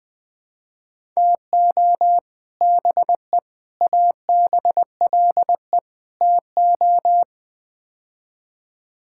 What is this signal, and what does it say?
Morse code, heard as a keyed tone, about 20 wpm, 700 Hz. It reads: TO BE ABLE TO
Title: Build A Phrase - 5-Word Phrases - 20wpm